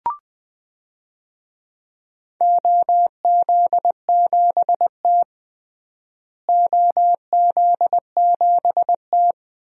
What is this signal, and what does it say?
Morse code, heard as a keyed tone, about 20 wpm, 700 Hz. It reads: OZ7T OZ7T
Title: DXCC-100 Call Signs - 4 Characters - Repeated 3x - 20wpm